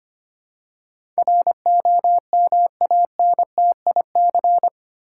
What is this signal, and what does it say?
Morse code, heard as a keyed tone, about 25 wpm, 700 Hz. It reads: ROMANTIC
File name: 8 Letter Words - First 3 Letters Given - 25wpm